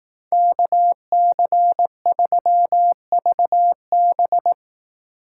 Morse code, heard as a keyed tone, about 19 wpm, 700 Hz. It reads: KC3VB
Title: Speed Racing - USA 2x2 Call Signs 15wpm